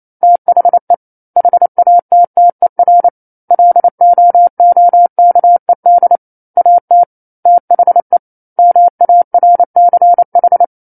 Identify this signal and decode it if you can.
Morse code, heard as a keyed tone, about 28 wpm, 700 Hz. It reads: THE HATTER LOOKED AT THE MARCH
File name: 09-Chapter-09 28 WPM